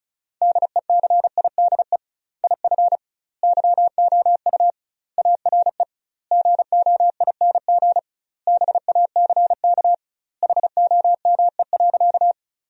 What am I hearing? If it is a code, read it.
Morse code, heard as a keyed tone, about 36 wpm, 700 Hz. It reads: DECIDE IF YOU ARE GOING BACK HOME.
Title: 2x Word-Spacing - Sentences from Top 500 Words - 35wpm